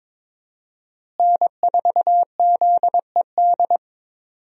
Morse code, heard as a keyed tone, about 21 wpm, 700 Hz. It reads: N4ZED
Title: USA Call Signs - 5 Characters - Repeated 3x - 22wpm